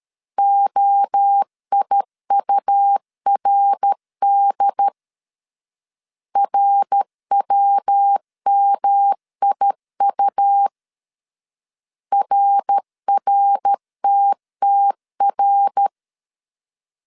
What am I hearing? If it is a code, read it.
Morse code, heard as a keyed tone, about 13 wpm, 800 Hz. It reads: OIURD RWMIU RRTTR
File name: CW / DET / 1 3 1500C S C 12wpm 2750